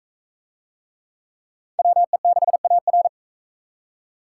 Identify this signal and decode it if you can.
Morse code, heard as a keyed tone, about 41 wpm, 700 Hz. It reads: WE6AR